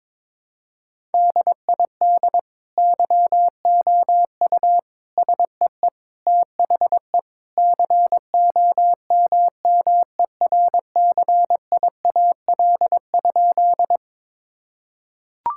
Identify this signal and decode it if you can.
Morse code, heard as a keyed tone, about 21 wpm, 700 Hz. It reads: DID YOU SEE THE COMMERCIAL?
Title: Sentences from Top 1000 Words - Review Part 5 - 22wpm